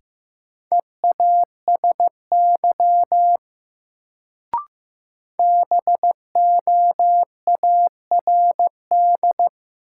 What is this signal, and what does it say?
Morse code, heard as a keyed tone, about 15 wpm, 700 Hz. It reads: EASY BOARD
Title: Sentences from Top 700 Words - New Word Drill - 15wpm